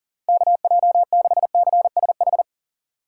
Morse code, heard as a keyed tone, about 49 wpm, 700 Hz. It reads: KJ6CSH
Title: Speed Racing - USA 2x3 Call Signs 40wpm